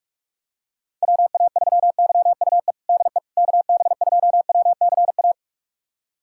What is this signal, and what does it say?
Morse code, heard as a keyed tone, about 46 wpm, 700 Hz. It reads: WA3YUE DE KB2WCA
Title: USA Callsign DE Callsign 45wpm